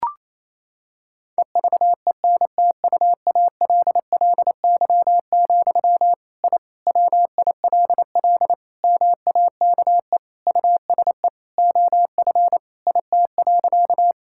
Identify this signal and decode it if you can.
Morse code, heard as a keyed tone, about 28 wpm, 700 Hz. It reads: EVENTUALLY, I WILL MAKE USE OF IT.
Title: Sentences from Top 1000 Words - Review Part 6 - 28wpm